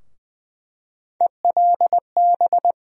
Morse code, heard as a keyed tone, about 20 wpm, 700 Hz. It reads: ELB
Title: USA Call Sign Postfixes - 3 Letters 20wpm